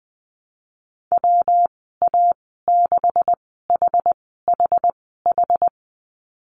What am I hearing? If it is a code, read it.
Morse code, heard as a keyed tone, about 20 wpm, spaced as for 15 wpm, 700 Hz. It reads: WA6HHH
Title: 161 Lesson 26g - Letter g - Call Signs 20wpm